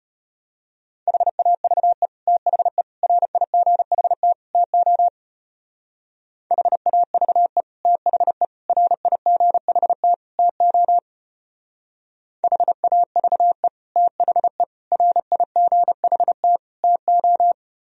Morse code, heard as a keyed tone, about 33 wpm, 700 Hz. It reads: HAVE THE RIGHT TO HAVE THE RIGHT TO HAVE THE RIGHT TO